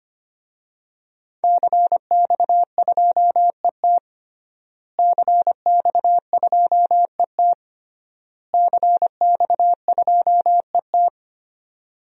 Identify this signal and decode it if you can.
Morse code, heard as a keyed tone, about 25 wpm, 700 Hz. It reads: CX2ET CX2ET CX2ET
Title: DXCC-100 Call Signs - 5 Characters - Repeated 3x - 25wpm